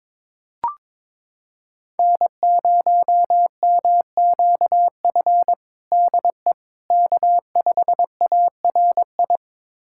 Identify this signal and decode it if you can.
Morse code, heard as a keyed tone, about 22 wpm, 700 Hz. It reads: N0MQF DE K5ARI
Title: USA Callsign DE Callsign 22wpm